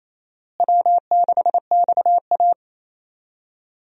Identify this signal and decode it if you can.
Morse code, heard as a keyed tone, about 28 wpm, 700 Hz. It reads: W6XA
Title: USA Call Signs - 4 Characters - Repeated 3x - 28wpm